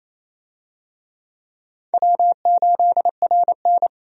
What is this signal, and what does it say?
Morse code, heard as a keyed tone, about 28 wpm, 700 Hz. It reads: W8RN
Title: USA Call Signs - 4 Characters - 28wpm